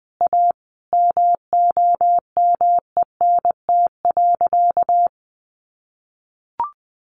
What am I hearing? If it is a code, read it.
Morse code, heard as a keyed tone, about 20 wpm, 700 Hz. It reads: A MOMENT.
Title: Sentences from Top 1000 Words 20wpm